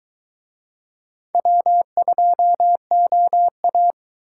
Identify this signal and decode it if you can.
Morse code, heard as a keyed tone, about 21 wpm, 700 Hz. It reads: W2OA